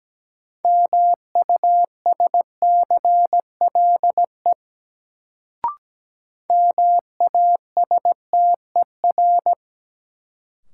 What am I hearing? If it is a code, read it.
Morse code, heard as a keyed tone, about 17 wpm, 700 Hz. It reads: MUSCLE MASTER
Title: Sentences from Top 1300 Words - New Word Drill Alt - 17wpm